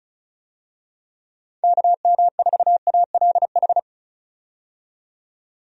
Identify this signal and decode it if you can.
Morse code, heard as a keyed tone, about 34 wpm, 700 Hz. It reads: KM4ALH
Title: USA Call Signs - 6 Characters - 35wpm